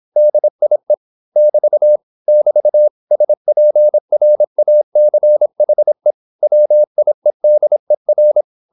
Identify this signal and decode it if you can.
Morse code, heard as a keyed tone, about 26 wpm, 600 Hz. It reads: DIE <BT> <BT> SPRACHE WIEDER